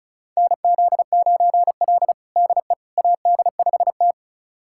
Morse code, heard as a keyed tone, about 36 wpm, 700 Hz. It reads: NZ9L DE AD5T
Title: USA Callsign DE Callsign - 4 Characters - 35wpm